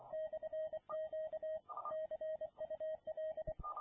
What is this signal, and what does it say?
Morse code, heard as a keyed tone, about 24 wpm, 630 Hz. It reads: /Q CUL